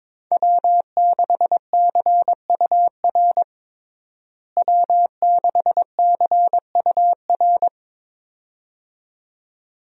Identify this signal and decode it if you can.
Morse code, heard as a keyed tone, about 22 wpm, 700 Hz. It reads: W6CUR W6CUR
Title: USA Call Signs - 5 Characters - Repeated 2x - 22wpm